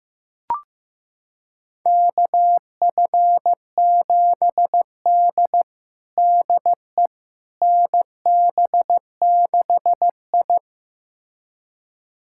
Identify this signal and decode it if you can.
Morse code, heard as a keyed tone, about 16 wpm, 700 Hz. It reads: KF7D DE NB6I
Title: USA Callsign DE Callsign - 4 Characters - 15wpm